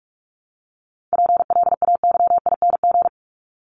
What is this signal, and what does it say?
Morse code, heard as a keyed tone, about 44 wpm, 700 Hz. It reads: PLAYING
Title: ING Suffix - Top 100 Words with Base Words 45wpm